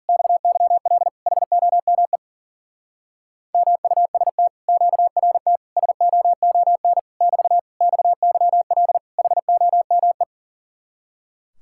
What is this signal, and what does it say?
Morse code, heard as a keyed tone, about 47 wpm, 700 Hz. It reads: XYL HOME MUST QRT SOON <BT> XYL HOME
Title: Speed-Racing Common QSO Phrases Encore - 40wpm